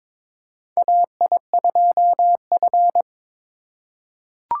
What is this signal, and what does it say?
Morse code, heard as a keyed tone, about 22 wpm, 700 Hz. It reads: AI2F
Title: USA Call Signs - 4 Characters - 4x Character Spacing - 22wpm